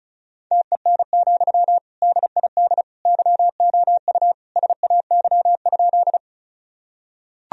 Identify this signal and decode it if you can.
Morse code, heard as a keyed tone, about 36 wpm, 700 Hz. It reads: TEN, DID YOU SAY?
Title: Sentences from Top 1000 Words - Review Part 2 - 35wpm